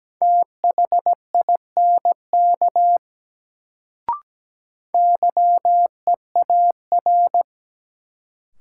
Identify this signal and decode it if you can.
Morse code, heard as a keyed tone, about 17 wpm, 700 Hz. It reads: THINK YEAR
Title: Sentences from Top 200 Words - New Word Drill - 17wpm